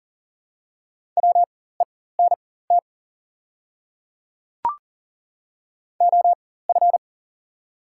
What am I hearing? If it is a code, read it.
Morse code, heard as a keyed tone, about 40 wpm, spaced as for 19 wpm, 690 Hz. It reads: WENT OF
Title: 4x Character Spacing - Top 200 Words - 40wpm